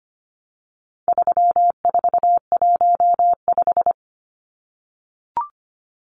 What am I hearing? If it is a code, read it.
Morse code, heard as a keyed tone, about 25 wpm, 700 Hz. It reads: 3415
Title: Numbers - 4-Digit 25wpm